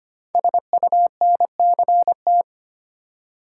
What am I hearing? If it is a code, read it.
Morse code, heard as a keyed tone, about 25 wpm, 700 Hz. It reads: SUNCT
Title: Code Groups - 5 Letter:Numbers - 25wpm